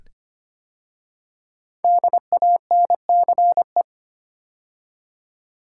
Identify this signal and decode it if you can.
Morse code, heard as a keyed tone, about 25 wpm, 700 Hz. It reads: DANCE
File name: 5 Letter Words - First 3 Letters Given - 25wpm